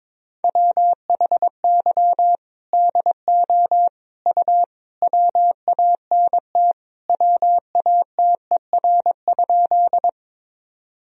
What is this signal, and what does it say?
Morse code, heard as a keyed tone, about 23 wpm, 700 Hz. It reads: WHY DO U WANT WATER?
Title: Sentences from Top 200 Words and Abbreviations 22wpm